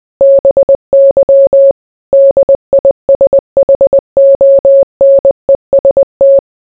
Morse code, heard as a keyed tone, about 20 wpm, 550 Hz. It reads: BY DISHONEST